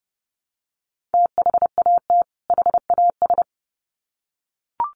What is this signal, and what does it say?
Morse code, heard as a keyed tone, about 30 wpm, 700 Hz. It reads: THAT HAS
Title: Sets of 2 Words 30wpm